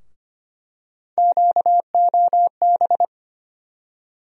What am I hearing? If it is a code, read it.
Morse code, heard as a keyed tone, about 25 wpm, 700 Hz. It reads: QOB
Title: USA Call Sign Postfixes - 3 Letters 25wpm